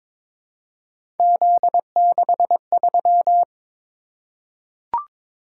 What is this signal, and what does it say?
Morse code, heard as a keyed tone, about 22 wpm, 700 Hz. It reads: Z63
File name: practice / 22wpm / Three Letter-Number Combinations - 22wpm